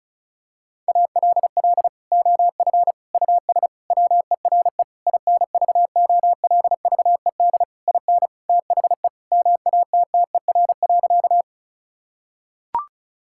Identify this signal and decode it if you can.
Morse code, heard as a keyed tone, about 36 wpm, 700 Hz. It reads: ALL OF US WERE INVOLVED IN THE MATTER.
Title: Sentences from Top 1000 Words - Review Part 5 - 35wpm